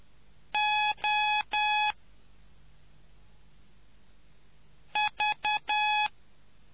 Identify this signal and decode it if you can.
Morse code, heard as a keyed tone, about 10 wpm, 840 Hz. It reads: O V